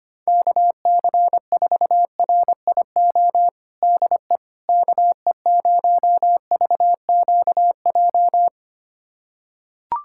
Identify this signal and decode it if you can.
Morse code, heard as a keyed tone, about 25 wpm, 700 Hz. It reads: KC4RIO DE KE0VQJ
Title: USA Callsign DE Callsign 25wpm